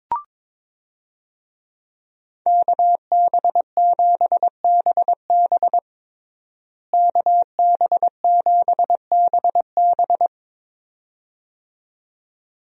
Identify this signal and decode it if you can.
Morse code, heard as a keyed tone, about 22 wpm, 700 Hz. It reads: KB7BB KB7BB
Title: USA Call Signs - 5 Characters - Repeated 2x - 22wpm